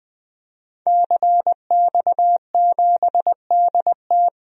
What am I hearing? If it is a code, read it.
Morse code, heard as a keyed tone, about 20 wpm, 700 Hz. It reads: CX7DT
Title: DXCC-100 Call Signs - 5 Characters - 20wpm